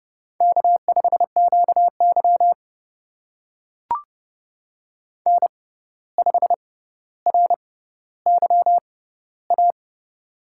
Morse code, heard as a keyed tone, about 30 wpm, spaced as for 14 wpm, 700 Hz. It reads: K5QY N5RYA
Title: USA Call Signs - 6x Character Spacing - 30wpm